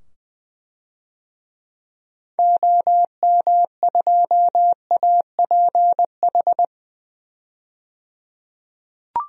Morse code, heard as a keyed tone, about 20 wpm, 700 Hz. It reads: OM2APH
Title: DXCC-100 Call Signs - 6 Characters - 20wpm